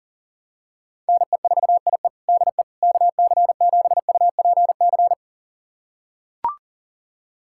Morse code, heard as a keyed tone, about 40 wpm, 700 Hz. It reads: NE4IE DE KC7UPC
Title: USA Callsign DE Callsign 40wpm